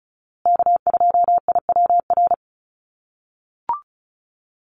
Morse code, heard as a keyed tone, about 34 wpm, 700 Hz. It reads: K2IWR
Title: USA Call Signs Repeated 3x 35wpm